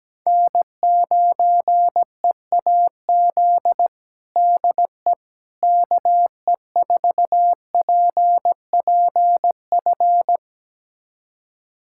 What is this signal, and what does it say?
Morse code, heard as a keyed tone, about 17 wpm, 700 Hz. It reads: N9EAZ DE KE4PPF
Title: USA Callsign DE Callsign 17wpm